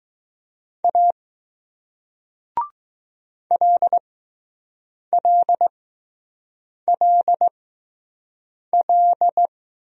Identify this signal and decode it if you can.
Morse code, heard as a keyed tone, about 20 wpm, spaced as for 6 wpm, 700 Hz. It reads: A LLLL